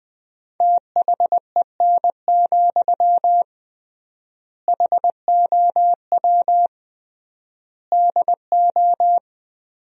Morse code, heard as a keyed tone, about 20 wpm, 700 Hz. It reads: THEN, HOW DO
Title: 3x Word-Spacing - Sentences from Top 100 Words - 20wpm